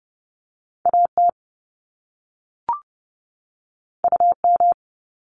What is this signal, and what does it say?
Morse code, heard as a keyed tone, about 30 wpm, 700 Hz. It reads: AT UM
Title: Top Two Letter Combinations in Words 30wpm